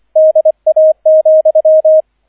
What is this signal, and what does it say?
Morse code, heard as a keyed tone, about 25 wpm, 630 Hz, with radio noise. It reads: DA,